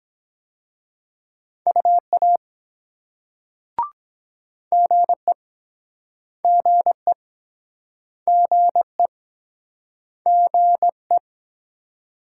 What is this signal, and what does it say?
Morse code, heard as a keyed tone, about 21 wpm, 700 Hz. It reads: UA GE GE GE GE
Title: Speed-Racing Multi-letter Phonemes - 17wpm